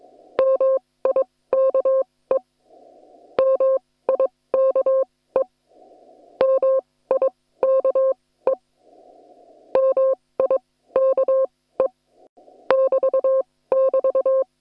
Morse code, heard as a keyed tone, about 22 wpm, spaced as for 16 wpm, 530 Hz. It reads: MIKE MIKE MIKE MIKE <BT><BT>
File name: attachments / hf 1 001